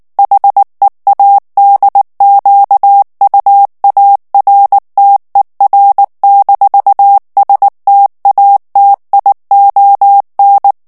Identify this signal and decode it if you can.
Morse code, heard as a keyed tone, about 19 wpm, 800 Hz. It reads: HEADQUARTER-STATION
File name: morse mp3 / ALLEMAND texte clair 2 vitesse 18 mots minute